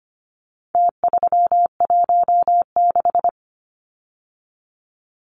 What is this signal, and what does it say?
Morse code, heard as a keyed tone, about 25 wpm, 700 Hz. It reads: T316
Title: Numbers - Contest Style - 4-Digit 25wpm